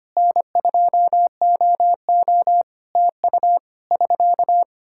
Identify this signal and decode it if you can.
Morse code, heard as a keyed tone, about 25 wpm, 700 Hz. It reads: N2OO TU <SK>
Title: K1USN SST QSO Practice 25wpm